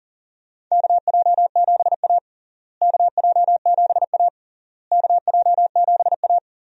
Morse code, heard as a keyed tone, about 40 wpm, 700 Hz. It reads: KJ7A KJ7A KJ7A
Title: USA Call Signs - 4 Characters - Repeated 3x - 40wpm